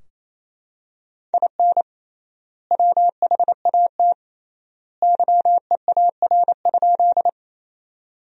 Morse code, heard as a keyed tone, about 28 wpm, 700 Hz. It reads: IN WHAT YEAR?
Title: 3x Word-Spacing - Sentences from Top 200 Words - 28wpm